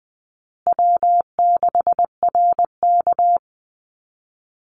Morse code, heard as a keyed tone, about 20 wpm, 700 Hz. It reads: W6RK
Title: Call Signs Repeated 2x - 20wpm